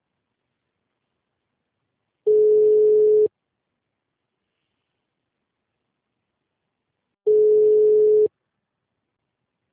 Ringback tone (Europe, 425 Hz).
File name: out- -2001- - - .129